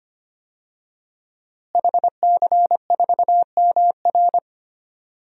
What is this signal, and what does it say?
Morse code, heard as a keyed tone, about 25 wpm, 700 Hz. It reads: HC4MR
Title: DXCC-100 Call Signs - 5 Characters - Repeated 2x - 25wpm